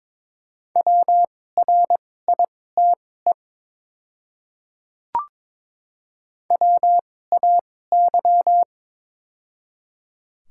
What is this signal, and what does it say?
Morse code, heard as a keyed tone, about 22 wpm, spaced as for 16 wpm, 700 Hz. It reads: WRITE WAY
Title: 2x Character Spacing - Top 100 Words - 22wpm